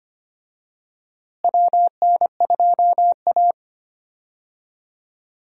Morse code, heard as a keyed tone, about 25 wpm, 700 Hz. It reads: WN2A
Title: USA Call Signs - 4 Characters - Repeated 2x - 25wpm